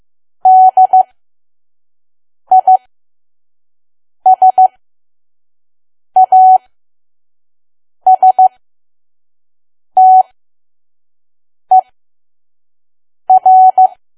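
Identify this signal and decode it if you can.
Morse code, heard as a keyed tone, about 15 wpm, spaced as for 5 wpm, 750 Hz. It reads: DISASTER
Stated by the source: Morse code practice at 5 WPM from KQ2P.COM.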